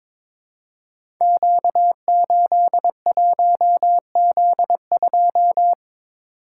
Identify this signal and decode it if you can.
Morse code, heard as a keyed tone, about 22 wpm, 700 Hz. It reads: Q81Z2
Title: Code Groups - 5 Letter:Numbers - 22wpm